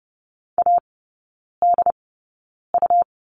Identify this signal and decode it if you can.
Morse code, heard as a keyed tone, about 30 wpm, spaced as for 9 wpm, 700 Hz. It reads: ADU
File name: Lesson 18e - Letter m - Three Characters 30wpm